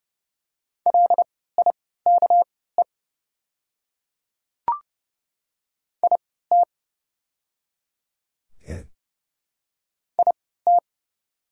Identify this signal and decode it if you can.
Morse code, heard as a keyed tone, about 30 wpm, spaced as for 17 wpm, 700 Hz. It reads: LIKE IT IT
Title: 3x Character Spacing - Top 100 Words - 30wpm